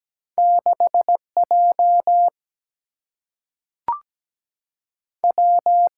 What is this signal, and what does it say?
Morse code, heard as a keyed tone, about 17 wpm, 700 Hz. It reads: 6 J W
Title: Single-Letter Single-Number 17wpm